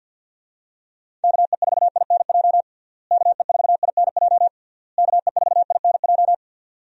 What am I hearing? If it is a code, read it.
Morse code, heard as a keyed tone, about 50 wpm, 700 Hz. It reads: KE4INJ KE4INJ KE4INJ
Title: USA Call Signs - 6 Characters - Repeated 3x - 50wpm